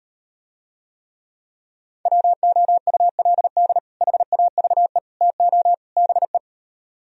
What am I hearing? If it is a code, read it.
Morse code, heard as a keyed tone, about 38 wpm, 700 Hz. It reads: WOULD HAVE TO BE